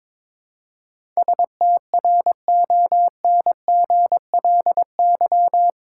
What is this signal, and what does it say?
Morse code, heard as a keyed tone, about 22 wpm, 700 Hz. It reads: STRONGLY
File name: practice / 22wpm / Eight Letter Words - 22wpm